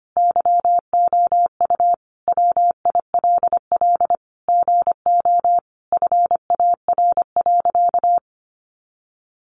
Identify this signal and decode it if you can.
Morse code, heard as a keyed tone, about 25 wpm, 700 Hz. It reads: YOU WILL GO FAR.
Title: Sentences from Top 400 Words 25wpm